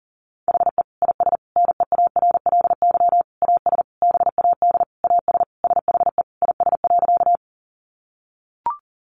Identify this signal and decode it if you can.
Morse code, heard as a keyed tone, about 40 wpm, 700 Hz. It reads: HE IS NEARLY AS BAD AS SHE IS.
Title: Sentences from Top 1300 Words 40wpm